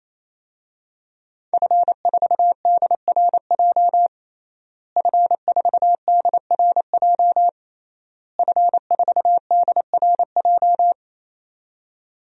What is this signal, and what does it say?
Morse code, heard as a keyed tone, about 28 wpm, 700 Hz. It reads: F4DRJ F4DRJ F4DRJ
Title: DXCC-100 Call Signs - 5 Characters - Repeated 3x - 28wpm